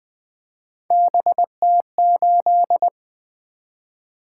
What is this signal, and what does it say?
Morse code, heard as a keyed tone, about 20 wpm, 700 Hz. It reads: BT8
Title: Three Letter-Number Combinations - 20wpm